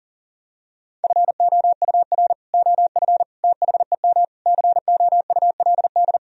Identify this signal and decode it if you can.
Morse code, heard as a keyed tone, about 40 wpm, 700 Hz. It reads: FOUR OF THEM COULD
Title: Sentences from Top 1000 Words - Review Part 1 - 40wpm